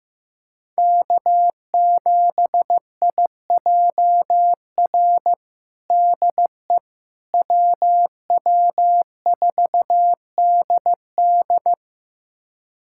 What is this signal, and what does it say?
Morse code, heard as a keyed tone, about 16 wpm, 700 Hz. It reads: K7IJR DE WW4DD
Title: USA Callsign DE Callsign - 5 Characters - 15wpm